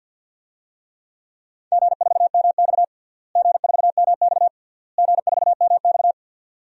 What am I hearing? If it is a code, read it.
Morse code, heard as a keyed tone, about 52 wpm, 700 Hz. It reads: G4MX G4MX G4MX
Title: DXCC-100 Call Signs - 4 Characters - Repeated 3x - 50wpm